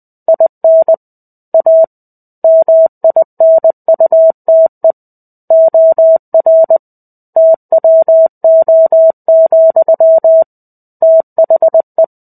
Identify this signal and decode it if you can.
Morse code, heard as a keyed tone, about 20 wpm, 650 Hz. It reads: IN A MINUTE OR TWO, THE
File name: 05-Chapter-05 20 WPM